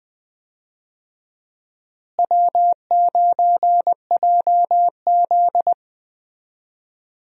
Morse code, heard as a keyed tone, about 20 wpm, 700 Hz. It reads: W9JZ
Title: USA Call Signs - 4 Characters - Repeated 2x - 20wpm